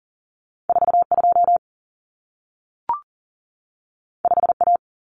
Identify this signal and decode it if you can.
Morse code, heard as a keyed tone, about 40 wpm, 700 Hz. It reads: 42 5A
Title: Common Cut Numbers - 2-Digit 40wpm